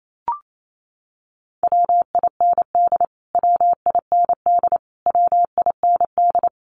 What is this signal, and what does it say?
Morse code, heard as a keyed tone, about 28 wpm, 700 Hz. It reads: WIND WIND WIND
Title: Top 500 Words Repeated 28wpm